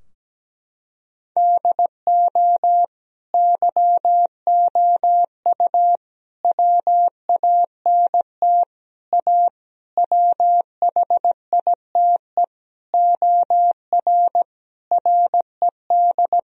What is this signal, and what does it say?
Morse code, heard as a keyed tone, about 17 wpm, 700 Hz. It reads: DO YOU WANT A WHITE OR RED
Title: Sentences from Top 1000 Words - Review Part 2 - 17wpm